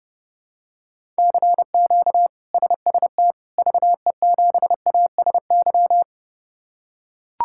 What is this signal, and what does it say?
Morse code, heard as a keyed tone, about 30 wpm, 700 Hz. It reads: CQ SST VE7ASY
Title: K1USN SST QSO Practice - 30wpm